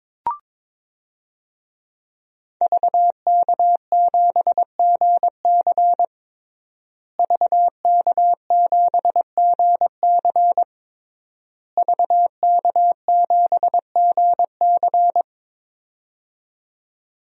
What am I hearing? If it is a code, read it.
Morse code, heard as a keyed tone, about 23 wpm, 700 Hz. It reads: VK7GC VK7GC VK7GC
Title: DXCC-100 Call Signs - 5 Characters - Repeated 3x - 22wpm